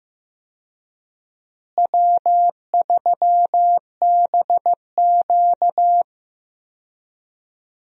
Morse code, heard as a keyed tone, about 15 wpm, 700 Hz. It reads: W3BQ
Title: USA Call Signs - 4 Characters - Repeated 2x - 15wpm